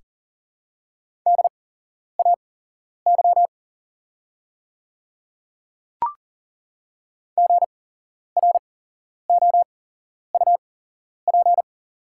Morse code, heard as a keyed tone, about 40 wpm, spaced as for 11 wpm, 690 Hz. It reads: DAY GROUP
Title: 8x Character Spacing - Top 300 Words - 40wpm